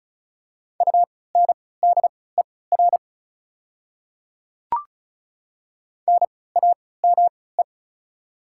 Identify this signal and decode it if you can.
Morse code, heard as a keyed tone, about 34 wpm, spaced as for 20 wpm, 700 Hz. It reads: UNDER NAME
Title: 3x Character Spacing - Top 300 Words - 35wpm